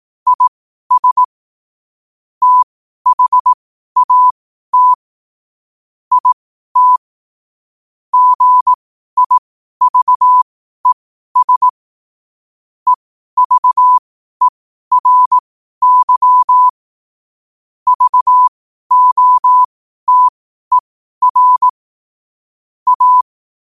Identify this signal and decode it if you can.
Morse code, heard as a keyed tone, about 18 wpm, spaced as for 12 wpm, 1000 Hz. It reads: IS THAT IT GIVES EVERY VOTER A
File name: qotdfiles / 12wpm